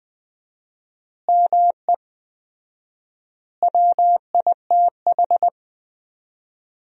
Morse code, heard as a keyed tone, about 20 wpm, 700 Hz. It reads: ME WITH
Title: 4x Word-Spacing - Sentences from Top 500 Words - 20wpm